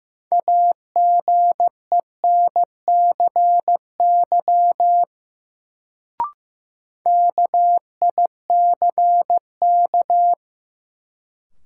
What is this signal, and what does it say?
Morse code, heard as a keyed tone, about 15 wpm, 700 Hz. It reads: AGENCY KICK
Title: Sentences from Top 1300 Words - New Word Drill Alt - 15wpm